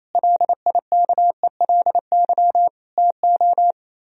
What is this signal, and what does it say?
Morse code, heard as a keyed tone, about 26 wpm, 700 Hz. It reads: LIKELY TO
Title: Speed-Racing - Sets of 3 Words 28wpm